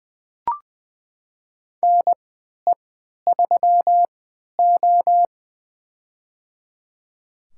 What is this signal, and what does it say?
Morse code, heard as a keyed tone, about 20 wpm, spaced as for 11 wpm, 700 Hz. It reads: NE3O
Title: USA Call Signs - 4 Characters - 3x Character Spacing - 20wpm